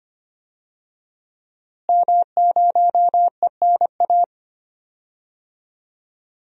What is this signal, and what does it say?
Morse code, heard as a keyed tone, about 25 wpm, 700 Hz. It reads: M0ENA
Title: DXCC-100 Call Signs - 5 Characters - Repeated 3x - 25wpm